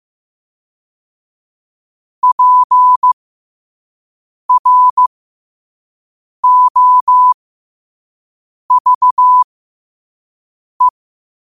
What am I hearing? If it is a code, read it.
Morse code, heard as a keyed tone, about 15 wpm, spaced as for 5 wpm, 1000 Hz. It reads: PROVE
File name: qotdfiles / 5wpm